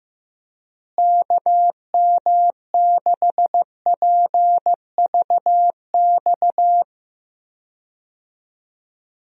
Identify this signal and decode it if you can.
Morse code, heard as a keyed tone, about 15 wpm, 700 Hz. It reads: KM6PVX
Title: USA Call Signs - 6 Characters - Repeated 3x - 15wpm